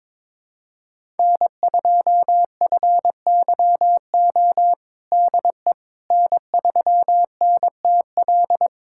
Morse code, heard as a keyed tone, about 22 wpm, 700 Hz. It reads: N2FYO DE N3NTL
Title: USA Callsign DE Callsign - 5 Characters - 22wpm